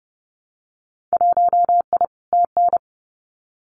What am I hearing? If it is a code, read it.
Morse code, heard as a keyed tone, about 30 wpm, 700 Hz. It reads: 1I TN
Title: Winter Field Day Exchanges Encore 30wpm